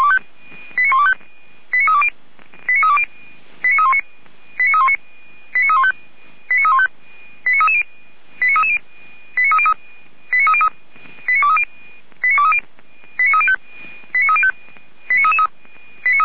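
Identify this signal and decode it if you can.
Five-tone selective call (ZVEI), heard as quick runs of five tones, tones of about 70 ms. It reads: 89125 89320 89320 89319 89319 89315 89315 89330 89330 89303 89303 89230 89230 89305 89305
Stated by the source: ZVEI five-tone selective call 89050 received by SDR.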